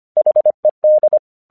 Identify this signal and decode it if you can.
Morse code, heard as a keyed tone, about 24 wpm, 610 Hz. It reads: HED
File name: 1 English 25wpm / AQC21